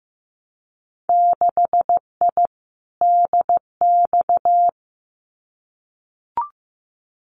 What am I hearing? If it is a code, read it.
Morse code, heard as a keyed tone, about 15 wpm, 700 Hz. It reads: 6I DX
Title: Winter Field Day Exchanges Encore 15wpm